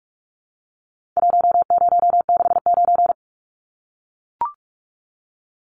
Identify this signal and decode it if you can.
Morse code, heard as a keyed tone, about 44 wpm, 700 Hz. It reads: 1069
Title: Numbers - Contest Style - 4-Digit 45wpm